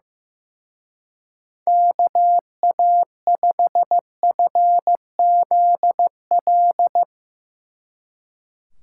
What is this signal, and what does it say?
Morse code, heard as a keyed tone, about 16 wpm, 700 Hz. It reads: KA5FZL
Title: Speed Racing - USA 2x3 Call Signs 15wpm